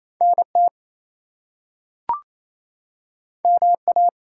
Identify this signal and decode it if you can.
Morse code, heard as a keyed tone, about 28 wpm, 700 Hz. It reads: NT MA
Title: US States and Canadian Provinces 28wpm